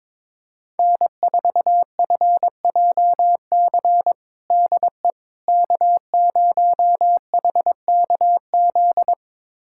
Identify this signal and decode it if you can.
Morse code, heard as a keyed tone, about 22 wpm, 700 Hz. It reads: N4FJC DE K0HKZ
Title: USA Callsign DE Callsign - 5 Characters - 22wpm